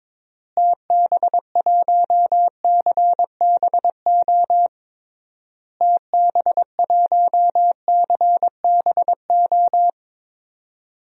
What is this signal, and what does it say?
Morse code, heard as a keyed tone, about 22 wpm, 700 Hz. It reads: TB1CBO TB1CBO
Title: DXCC-100 Call Signs - 6 Characters - Repeated 3x - 22wpm